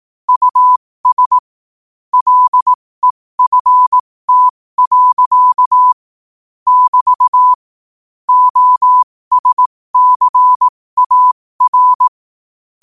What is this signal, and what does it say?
Morse code, heard as a keyed tone, about 18 wpm, spaced as for 15 wpm, 1000 Hz. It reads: US LEFT. <BT> OSCAR
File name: qotdfiles / 15wpm